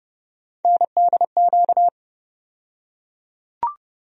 Morse code, heard as a keyed tone, about 30 wpm, 700 Hz. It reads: NDQ
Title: USA Call Sign Postfixes - 3 Letters 30wpm